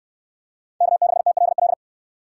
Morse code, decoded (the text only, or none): K6TLL